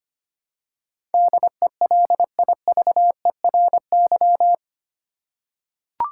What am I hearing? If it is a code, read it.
Morse code, heard as a keyed tone, about 25 wpm, 700 Hz. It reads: DELIVERY
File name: 8 Letter Words - First 3 Letters Given - 25wpm